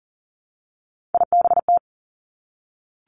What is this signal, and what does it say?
Morse code, heard as a keyed tone, about 42 wpm, 700 Hz. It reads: IBT
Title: Three Challenging Letter-Number Combinations 40wpm